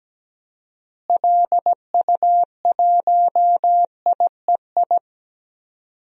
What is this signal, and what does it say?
Morse code, heard as a keyed tone, about 17 wpm, 700 Hz. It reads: LU1IEI
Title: DXCC-100 Call Signs - 6 Characters - Repeated 2x - 17wpm